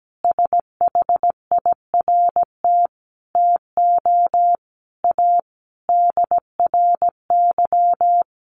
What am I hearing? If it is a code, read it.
Morse code, heard as a keyed tone, about 17 wpm, 700 Hz. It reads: SHIRT TO A DRY
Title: Additional Easy Phrases 17wpm